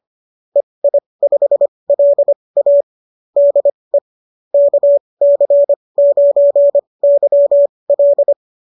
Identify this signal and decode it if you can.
Morse code, heard as a keyed tone, about 25 wpm, spaced as for 20 wpm, 570 Hz. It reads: EI5LA DE KC9YL